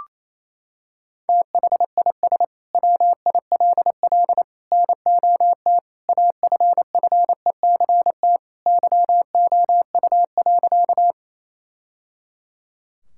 Morse code, heard as a keyed tone, about 28 wpm, 700 Hz. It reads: THIS WILL NOT AFFECT YOU.
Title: Sentences from Top 1000 Words - Review Part 6 - 28wpm